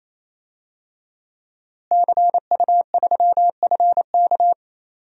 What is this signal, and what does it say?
Morse code, heard as a keyed tone, about 28 wpm, 700 Hz. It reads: CU3FK
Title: DXCC-100 Call Signs - 5 Characters - Repeated 3x - 28wpm